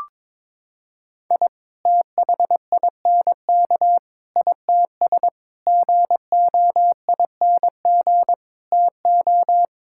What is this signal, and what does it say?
Morse code, heard as a keyed tone, about 22 wpm, 700 Hz. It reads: I THINK ITS GOING TO
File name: practice / 22wpm / Sets of 6 Words - 22wpm